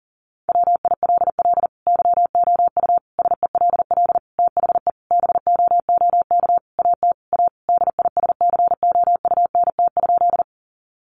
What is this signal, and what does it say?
Morse code, heard as a keyed tone, about 40 wpm, 700 Hz. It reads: WILL YOU SELL THE BOOK AT A DISCOUNT?
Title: Sentences from Top 1200 Words 40wpm